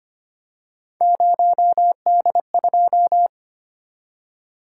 Morse code, heard as a keyed tone, about 25 wpm, 700 Hz. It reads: 0D2
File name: Three Letter-Number Combinations - 25wpm